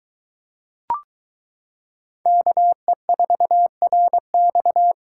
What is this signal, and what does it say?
Morse code, heard as a keyed tone, about 22 wpm, 700 Hz. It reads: KE4RX